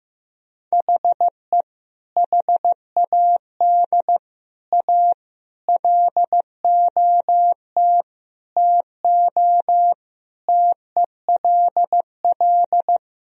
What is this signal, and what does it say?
Morse code, heard as a keyed tone, about 16 wpm, 700 Hz. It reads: HE HAD A LOT TO TELL
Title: Sentences from Top 1000 Words - Review Part 3 - 15wpm